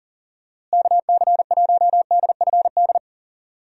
Morse code, heard as a keyed tone, about 40 wpm, 700 Hz. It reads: KC1DFD
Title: USA Call Signs - 6 Characters - Repeated 2x - 40wpm